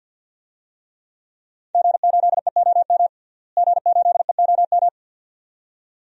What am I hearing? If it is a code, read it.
Morse code, heard as a keyed tone, about 47 wpm, 700 Hz. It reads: G8EOM G8EOM
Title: DXCC-100 Call Signs - 5 Characters - Repeated 2x - 50wpm